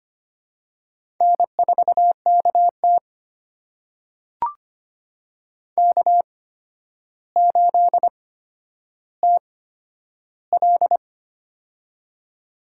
Morse code, heard as a keyed tone, about 25 wpm, spaced as for 11 wpm, 700 Hz. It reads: N4KT K8TL
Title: USA Call Signs - 4 Characters - 8x Character Spacing - 25wpm